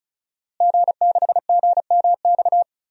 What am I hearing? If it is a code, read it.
Morse code, heard as a keyed tone, about 36 wpm, 700 Hz. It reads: G6GMX
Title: DXCC-100 Call Signs - 5 Characters - Repeated 3x - 35wpm